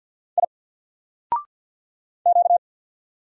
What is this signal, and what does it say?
Morse code, heard as a keyed tone, about 48 wpm, 700 Hz. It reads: I Q
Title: Single Character (Letter:Number:Punctuation:Prosign) 50wpm